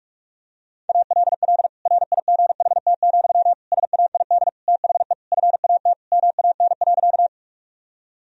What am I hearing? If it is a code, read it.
Morse code, heard as a keyed tone, about 41 wpm, 700 Hz. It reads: ALL RIGHT, SAID THE FAT MAN.
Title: Sentences from Top 1000 Words - Review Part 4 - 45wpm